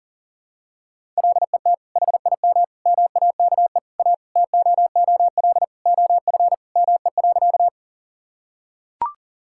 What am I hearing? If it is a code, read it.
Morse code, heard as a keyed tone, about 40 wpm, 700 Hz. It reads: LET HIM MAKE A TOOL OF ME.
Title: Sentences from Top 1000 Words - Review Part 6 - 40wpm